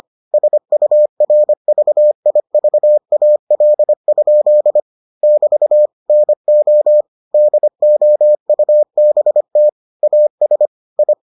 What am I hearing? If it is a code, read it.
Morse code, heard as a keyed tone, about 25 wpm, 600 Hz. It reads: SURVIVAL? <BT> NO DOUBT AS I